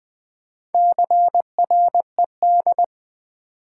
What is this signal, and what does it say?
Morse code, heard as a keyed tone, about 20 wpm, 700 Hz. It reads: CRED
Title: Build A Word - 4 Letters At A Time - 11-Letter Words - 20wpm